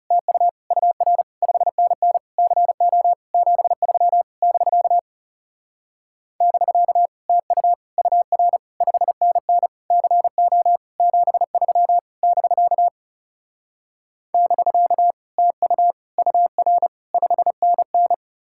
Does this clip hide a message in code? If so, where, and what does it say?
Yes: Morse code from 0.1 to 18.1 s, heard as a keyed tone, about 40 wpm, 700 Hz. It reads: TU UR 5NN CO 73 <BK> <BK> TU UR 5NN CO 73 <BK> <BK> TU UR 5NN